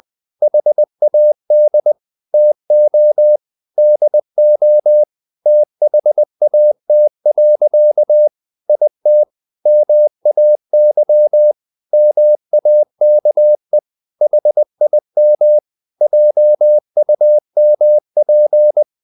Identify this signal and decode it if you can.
Morse code, heard as a keyed tone, about 20 wpm, 600 Hz. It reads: HAD TO DO THAT. IT MAY MAKE HIM JUMP
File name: the old man and the sea part 2 20wpm